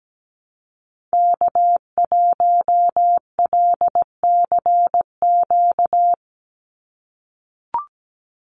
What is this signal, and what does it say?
Morse code, heard as a keyed tone, about 17 wpm, 700 Hz. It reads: K1LCQ
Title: USA Call Signs 17wpm